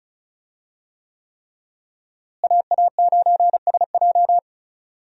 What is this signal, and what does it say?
Morse code, heard as a keyed tone, about 36 wpm, 700 Hz. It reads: AA9SJ
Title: USA Call Signs - 5 Characters - Repeated 3x - 35wpm